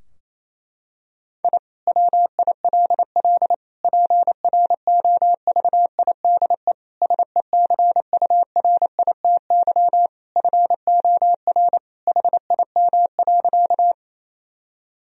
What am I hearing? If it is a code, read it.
Morse code, heard as a keyed tone, about 27 wpm, 700 Hz. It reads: I WILL PROVIDE SECURITY FOR HIM.
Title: Sentences from Top 1000 Words - Review Part 4 - 28wpm